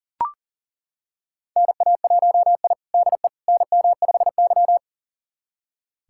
Morse code, heard as a keyed tone, about 40 wpm, 700 Hz. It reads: NA1I DE NM5Y
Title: USA Callsign DE Callsign - 4 Characters - 40wpm